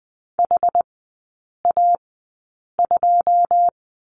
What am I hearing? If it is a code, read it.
Morse code, heard as a keyed tone, about 20 wpm, spaced as for 8 wpm, 700 Hz. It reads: HA2
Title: 081 Lesson 14e - Number 2 - Three Characters 20wpm